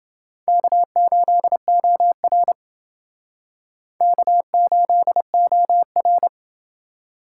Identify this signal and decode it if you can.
Morse code, heard as a keyed tone, about 29 wpm, 700 Hz. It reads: K8OR K8OR